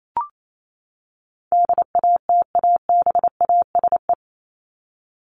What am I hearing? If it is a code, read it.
Morse code, heard as a keyed tone, about 28 wpm, 700 Hz. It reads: DATABASE
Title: Top 1751-2000 Words 28wpm